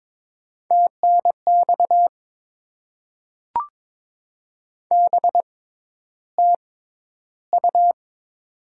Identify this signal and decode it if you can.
Morse code, heard as a keyed tone, about 22 wpm, spaced as for 11 wpm, 700 Hz. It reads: TNX BTU
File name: Top 20 QSO Elements - 6x Character Spacing - 22wpm